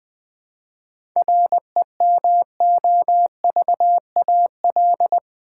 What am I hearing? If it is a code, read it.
Morse code, heard as a keyed tone, about 20 wpm, 700 Hz. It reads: REMOVAL
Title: 7 Letter Words - First 3 Letters Given - 20wpm